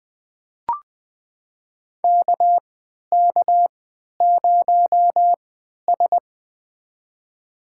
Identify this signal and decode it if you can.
Morse code, heard as a keyed tone, about 20 wpm, spaced as for 11 wpm, 700 Hz. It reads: KK0S
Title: USA Call Signs - 4 Characters - 3x Character Spacing - 20wpm